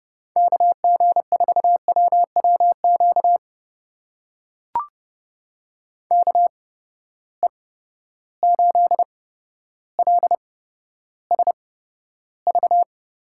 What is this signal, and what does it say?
Morse code, heard as a keyed tone, about 30 wpm, spaced as for 13 wpm, 700 Hz. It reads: KG4WWQ KE8LSV
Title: USA Call Signs - 6 Characters - 8x Character Spacing - 30wpm